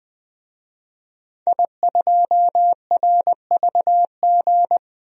Morse code, heard as a keyed tone, about 20 wpm, 700 Hz. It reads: I2RVG
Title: DXCC-100 Call Signs - 5 Characters - Repeated 3x - 20wpm